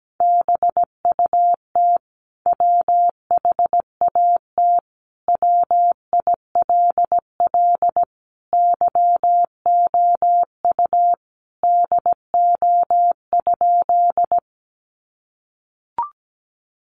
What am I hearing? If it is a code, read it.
Morse code, heard as a keyed tone, about 17 wpm, 700 Hz. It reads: BUT WHAT WILL YOU DO?
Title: Sentences from Top 100 Words 17wpm